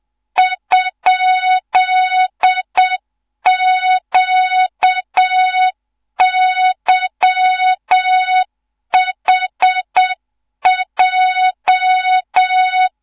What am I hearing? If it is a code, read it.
Morse code, heard as a keyed tone, about 7 wpm, 3700 Hz, with noise. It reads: ?QYHJ
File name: Morze-HA2MN / hang / 14 280 HA2MN